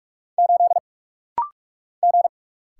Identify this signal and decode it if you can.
Morse code, heard as a keyed tone, about 48 wpm, 700 Hz. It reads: 8 G
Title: Single Letter-Number - Rapid-Fire 45wpm